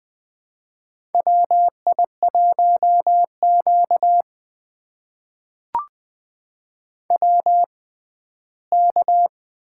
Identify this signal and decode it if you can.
Morse code, heard as a keyed tone, about 20 wpm, spaced as for 14 wpm, 700 Hz. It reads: WI1Q WK
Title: USA Call Signs - 4 Characters - 6x Character Spacing - 20wpm